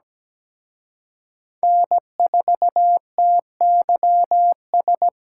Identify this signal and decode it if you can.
Morse code, heard as a keyed tone, about 20 wpm, 700 Hz. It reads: N4TYS
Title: Speed Racing - USA 1x3 Call Signs 17wpm